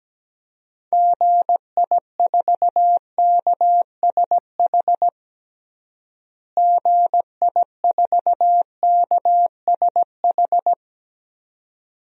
Morse code, heard as a keyed tone, about 17 wpm, 700 Hz. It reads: GI4KSH GI4KSH
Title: DXCC-100 Call Signs - 6 Characters - Repeated 2x - 17wpm